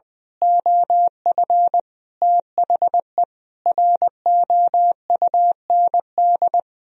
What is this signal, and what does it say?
Morse code, heard as a keyed tone, about 20 wpm, 700 Hz. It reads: OF THE ROUND
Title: Sentences from Top 1000 Words - Review Part 2 - 20wpm